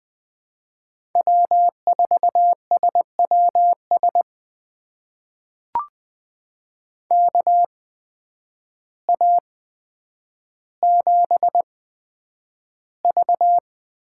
Morse code, heard as a keyed tone, about 20 wpm, spaced as for 9 wpm, 700 Hz. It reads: W4SWS KA7V
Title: USA Call Signs - 8x Character Spacing - 20wpm